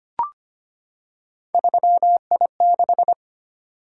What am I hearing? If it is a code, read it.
Morse code, heard as a keyed tone, about 25 wpm, 700 Hz. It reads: 3I6
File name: Three Letter-Number Combinations - 25wpm